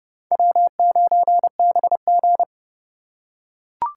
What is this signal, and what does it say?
Morse code, heard as a keyed tone, about 30 wpm, 700 Hz. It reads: W9BG
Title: USA Call Signs - 4 Characters - 3x Character Spacing - 30wpm